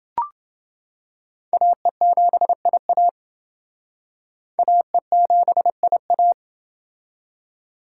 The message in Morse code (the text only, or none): AE7IA AE7IA